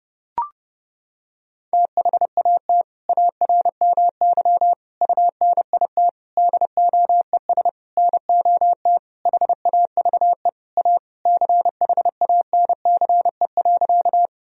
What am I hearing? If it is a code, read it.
Morse code, heard as a keyed tone, about 30 wpm, 700 Hz. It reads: THAT ARMY UNIT DOES NOT HAVE A CHANCE.
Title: Sentences from Top 1000 Words - Review Part 2 - 30wpm